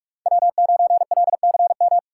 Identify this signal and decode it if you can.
Morse code, heard as a keyed tone, about 46 wpm, 700 Hz. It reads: W9LCM
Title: USA Call Signs - 5 Characters - 2x Character Spacing - 45wpm